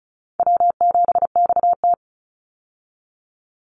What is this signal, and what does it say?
Morse code, heard as a keyed tone, about 34 wpm, 700 Hz. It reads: W7XT
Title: Easy Call Signs Repeated 2x - 35wpm